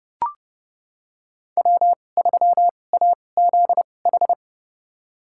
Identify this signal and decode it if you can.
Morse code, heard as a keyed tone, about 30 wpm, spaced as for 22 wpm, 700 Hz. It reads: W3AZH
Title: USA Call Signs - 5 Characters - 2x Character Spacing - 30wpm